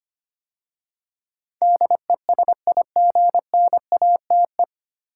Morse code, heard as a keyed tone, about 25 wpm, 700 Hz. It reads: DESIGNATE
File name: practice / 25wpm / 9 Letter Words - First 3 Letters Given - 25wpm